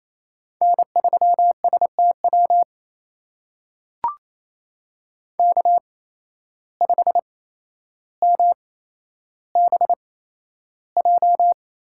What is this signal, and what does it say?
Morse code, heard as a keyed tone, about 28 wpm, spaced as for 12 wpm, 700 Hz. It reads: N3STW K5MBJ
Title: USA Call Signs - 5 Characters - 8x Character Spacing - 28wpm